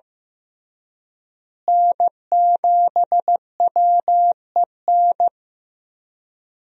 Morse code, heard as a keyed tone, about 15 wpm, 700 Hz. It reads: N7WEN
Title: USA Call Signs - 5 Characters - Repeated 3x - 15wpm